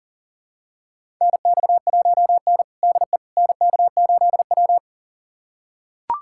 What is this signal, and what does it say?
Morse code, heard as a keyed tone, about 40 wpm, 690 Hz. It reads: NX1N DE NK8W
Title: USA Callsign DE Callsign - 4 Characters - 40wpm